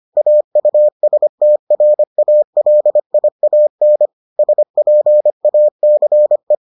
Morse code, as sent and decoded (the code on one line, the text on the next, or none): .- ..- ... - .-. .- .-.. .. .- -. / ... .--. .- -.-. .
AUSTRALIAN SPACE